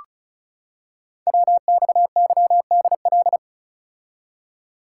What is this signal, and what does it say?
Morse code, heard as a keyed tone, about 36 wpm, 700 Hz. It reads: WXYDL
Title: Code Groups - 5 Letters - 35wpm